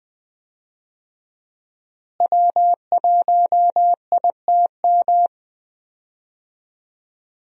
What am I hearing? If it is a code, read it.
Morse code, heard as a keyed tone, about 20 wpm, 700 Hz. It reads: W1ITM
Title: USA Call Signs - 5 Characters - Repeated 2x - 20wpm